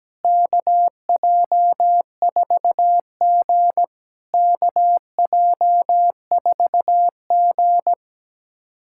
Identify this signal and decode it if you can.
Morse code, heard as a keyed tone, about 17 wpm, 700 Hz. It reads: KJ4G KJ4G
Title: SOTA QSO Practice 17wpm